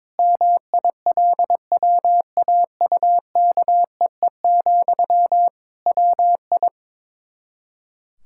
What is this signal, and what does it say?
Morse code, heard as a keyed tone, about 22 wpm, 700 Hz. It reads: MILWAUKEE, WI
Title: K1USN SST QSO Practice 22wpm